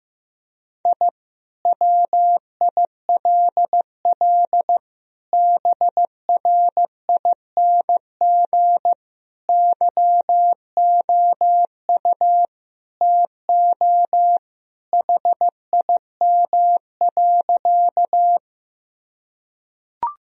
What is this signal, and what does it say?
Morse code, heard as a keyed tone, about 15 wpm, 700 Hz. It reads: I WILL BRING YOU TO HIM.
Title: Sentences from Top 1000 Words - Review Part 3 - 15wpm